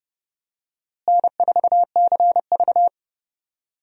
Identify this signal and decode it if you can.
Morse code, heard as a keyed tone, about 32 wpm, 700 Hz. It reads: N4CV
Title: USA Call Signs - 4 Characters - Repeated 3x - 30wpm